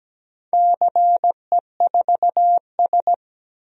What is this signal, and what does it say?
Morse code, heard as a keyed tone, about 17 wpm, 700 Hz. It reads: CE4S
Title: DXCC-100 Call Signs - 4 Characters - 17wpm